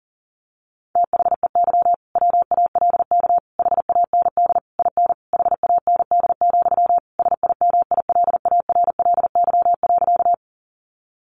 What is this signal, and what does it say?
Morse code, heard as a keyed tone, about 40 wpm, 700 Hz. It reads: THEY WALK HAND IN HAND, SIMILARLY.
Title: Sentences from Top 1400 Words 40wpm